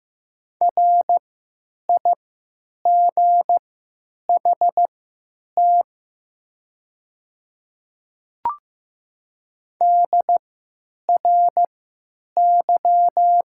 Morse code, heard as a keyed tone, about 15 wpm, spaced as for 9 wpm, 700 Hz. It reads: RIGHT DRY
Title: 3x Character Spacing - Top 500 Words - 15wpm